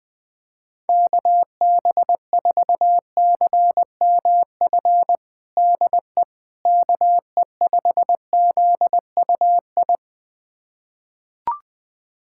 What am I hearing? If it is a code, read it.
Morse code, heard as a keyed tone, about 20 wpm, 700 Hz. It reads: KB4CMF DE KE5ZUI
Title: USA Callsign DE Callsign 20wpm